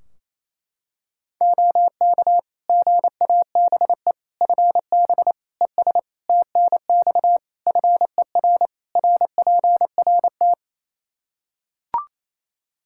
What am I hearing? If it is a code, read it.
Morse code, heard as a keyed tone, about 27 wpm, 700 Hz. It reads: OK GABE FB ES TNX FER RPRT
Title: Practice QSO - Exchange 2 - Info - 28wpm